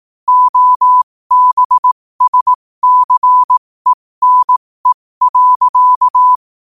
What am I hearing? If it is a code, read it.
Morse code, heard as a keyed tone, about 18 wpm, spaced as for 15 wpm, 1000 Hz. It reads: OBSCENE.
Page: New quotes every day in morse code at 15 Words per minute.